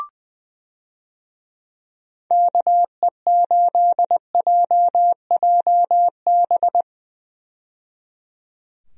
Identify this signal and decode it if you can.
Morse code, heard as a keyed tone, about 21 wpm, 700 Hz. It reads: KE8JJB
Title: USA Call Signs - 6 Characters - 20wpm